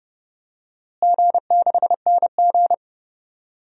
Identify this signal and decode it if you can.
Morse code, heard as a keyed tone, about 30 wpm, 700 Hz. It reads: G6NG
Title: DXCC-100 Call Signs - 4 Characters - Repeated 3x - 30wpm